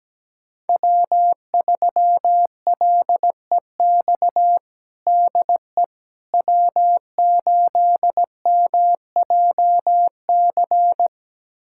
Morse code, heard as a keyed tone, about 17 wpm, 700 Hz. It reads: W3LEX DE W8MJC
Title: USA Callsign DE Callsign - 5 Characters - 17wpm